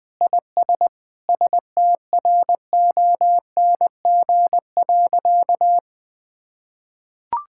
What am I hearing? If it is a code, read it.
Morse code, heard as a keyed tone, about 21 wpm, 700 Hz. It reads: IS STRONG.
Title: Sentences with 3-Word Phrases - Spoken Beginning - 20wpm